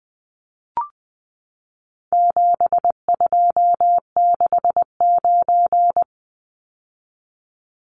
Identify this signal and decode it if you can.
Morse code, heard as a keyed tone, about 20 wpm, 700 Hz. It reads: 7269
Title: Numbers - 4-Digit 20wpm